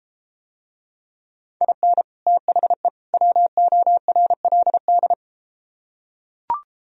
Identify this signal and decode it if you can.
Morse code, heard as a keyed tone, about 33 wpm, 700 Hz. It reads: IN THE WORLD